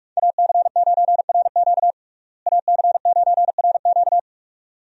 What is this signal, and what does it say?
Morse code, heard as a keyed tone, about 43 wpm, 700 Hz. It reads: AC9RQ AC9RQ
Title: USA Call Signs - 5 Characters - Repeated 2x - 45wpm